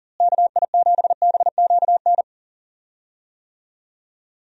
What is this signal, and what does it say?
Morse code, heard as a keyed tone, about 40 wpm, 700 Hz. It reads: KI7BQN
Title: USA Call Signs - 6 Characters - Repeated 3x - 40wpm